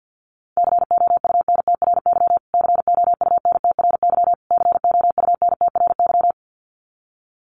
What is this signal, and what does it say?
Morse code, heard as a keyed tone, about 50 wpm, 700 Hz. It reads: COUNTRY COUNTRY COUNTRY
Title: Top 500 Words Repeated 50wpm